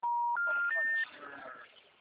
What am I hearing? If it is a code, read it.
Special information tone, with speech.